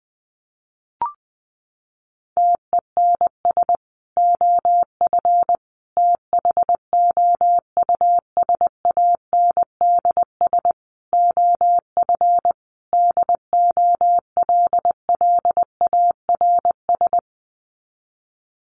Morse code, heard as a keyed tone, about 20 wpm, 700 Hz. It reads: TENS OF THOUSANDS OF DOLLARS
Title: Sets of 5 Words Encore2 20wpm